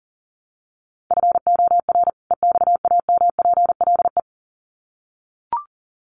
Morse code, heard as a keyed tone, about 40 wpm, 700 Hz. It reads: FOR EXAMPLE
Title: Sets of 2 Words 40wpm